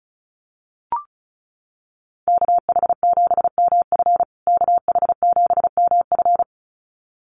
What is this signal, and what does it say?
Morse code, heard as a keyed tone, about 34 wpm, 700 Hz. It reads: KH7MF KH7MF
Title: USA Call Signs Repeated 2x 35wpm